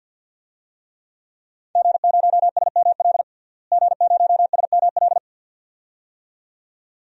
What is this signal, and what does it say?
Morse code, heard as a keyed tone, about 51 wpm, 700 Hz. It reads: G0SML G0SML
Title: DXCC-100 Call Signs - 5 Characters - Repeated 2x - 50wpm